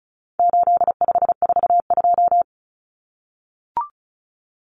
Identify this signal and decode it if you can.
Morse code, heard as a keyed tone, about 34 wpm, 700 Hz. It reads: 8542
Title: Common Cut Numbers - 4-Digit 35wpm